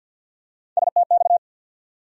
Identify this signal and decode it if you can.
Morse code, heard as a keyed tone, about 52 wpm, 700 Hz. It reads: STX
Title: USA Call Sign Postfixes - 3 Letters 50wpm